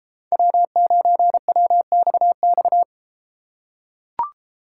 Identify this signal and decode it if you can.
Morse code, heard as a keyed tone, about 33 wpm, 700 Hz. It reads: W9WXX